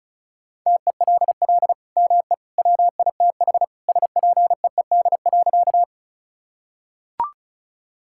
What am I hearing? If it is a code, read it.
Morse code, heard as a keyed tone, about 36 wpm, 700 Hz. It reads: TELL ME WITH SPEED.
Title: Sentences from Top 1000 Words - Review Part 5 - 35wpm